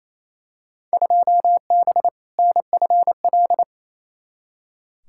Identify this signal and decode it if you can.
Morse code, heard as a keyed tone, about 28 wpm, 700 Hz. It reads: 2B NFL
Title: Field Day Exchanges Encore - 28wpm